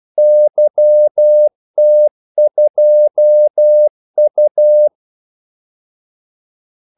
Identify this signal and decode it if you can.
Morse code, heard as a keyed tone, about 12 wpm, 600 Hz. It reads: YT2U
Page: 2 ekstra ordmellemrum mellem kaldesignalerne